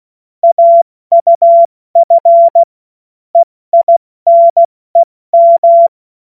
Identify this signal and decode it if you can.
Morse code, heard as a keyed tone, about 16 wpm, 680 Hz. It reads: AUF EINEM
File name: Witz-16-September-2025-70-F80